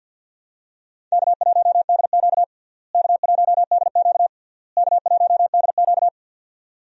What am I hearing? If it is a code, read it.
Morse code, heard as a keyed tone, about 50 wpm, 700 Hz. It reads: K1DQ K1DQ K1DQ
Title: USA Call Signs - 4 Characters - Repeated 3x - 50wpm